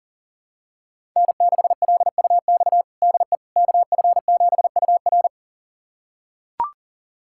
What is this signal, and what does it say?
Morse code, heard as a keyed tone, about 37 wpm, 700 Hz. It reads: N6LUX DE KF7UR
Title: USA Callsign DE Callsign - 5 Characters - 40wpm